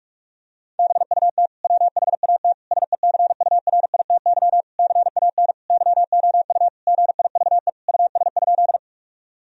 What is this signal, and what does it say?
Morse code, heard as a keyed tone, about 46 wpm, 700 Hz. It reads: BUT WHAT SECURITY CAN YOU GIVE US?
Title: Sentences from Top 1000 Words - Review Part 4 - 45wpm